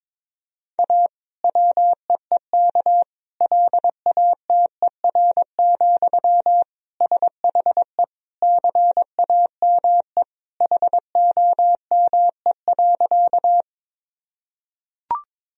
Morse code, heard as a keyed tone, about 21 wpm, 700 Hz. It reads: A WEEK LATER, SHE CAME HOME.
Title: Sentences from Top 1000 Words - Review Part 4 - 22wpm